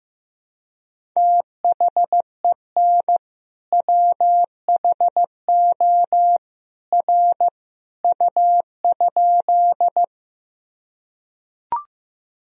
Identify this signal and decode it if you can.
Morse code, heard as a keyed tone, about 15 wpm, 700 Hz. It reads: THEN WHO R U?
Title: Sentences from Top 100 Words and Abbreviations 15wpm